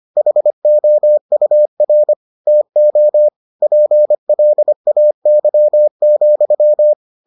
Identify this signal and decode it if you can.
Morse code, heard as a keyed tone, about 25 wpm, 600 Hz. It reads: HOUR TO PLAY,